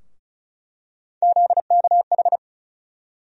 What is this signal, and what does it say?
Morse code, heard as a keyed tone, about 36 wpm, 700 Hz. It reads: ZKH
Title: USA Call Sign Postfixes - 3 Letters 35wpm